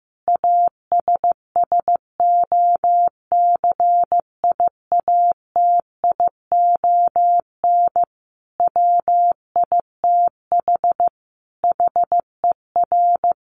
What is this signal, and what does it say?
Morse code, heard as a keyed tone, about 15 wpm, 700 Hz. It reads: ASSOCIATION WITH HER
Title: Sentences from Top 1100 Words 15wpm